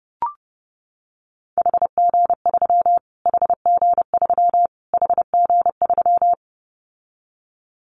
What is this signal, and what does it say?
Morse code, heard as a keyed tone, about 30 wpm, 700 Hz. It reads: HG3 HG3 HG3
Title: Callsign Prefixes for Top 25 DXCC - Repeated 3x - 30wpm